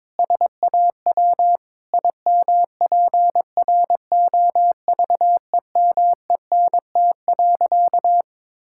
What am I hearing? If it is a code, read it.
Morse code, heard as a keyed tone, about 22 wpm, 700 Hz. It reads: SAW IMPROVEMENT.
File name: practice / 22wpm / Sentences with 8-Word Phrases - 22wpm